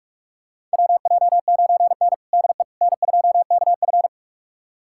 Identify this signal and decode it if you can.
Morse code, heard as a keyed tone, about 41 wpm, 690 Hz. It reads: WJ9N DE N2KF
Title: USA Callsign DE Callsign - 4 Characters - 45wpm